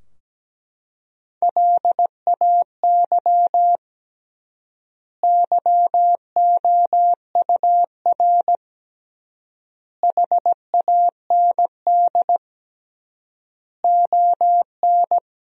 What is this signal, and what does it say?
Morse code, heard as a keyed tone, about 17 wpm, 700 Hz. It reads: LAY YOUR HAND ON
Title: 3x Word-Spacing - Sentences from Top 400 Words - 17wpm